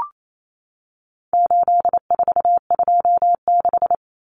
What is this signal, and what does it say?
Morse code, heard as a keyed tone, about 28 wpm, 700 Hz. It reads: 8426
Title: Numbers - 4-Digit 28wpm